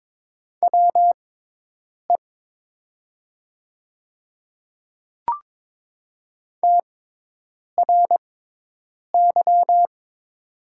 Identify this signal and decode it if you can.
Morse code, heard as a keyed tone, about 22 wpm, spaced as for 8 wpm, 700 Hz. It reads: WE TRY
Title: 6x Character Spacing - Top 200 Words - 22wpm